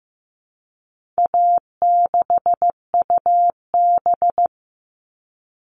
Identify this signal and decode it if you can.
Morse code, heard as a keyed tone, about 15 wpm, 700 Hz. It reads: A6UB
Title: All Cut Numbers - 4-Digit 15wpm